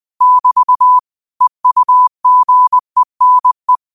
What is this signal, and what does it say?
Morse code, heard as a keyed tone, about 20 wpm, 1000 Hz. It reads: <BT> EUGENE
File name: qotdfiles / 20wpm